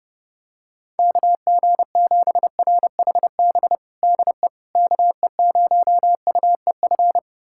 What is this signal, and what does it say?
Morse code, heard as a keyed tone, about 32 wpm, 700 Hz. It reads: KG7RHB DE KE0UEF
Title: USA Callsign DE Callsign - 6 Characters - 30wpm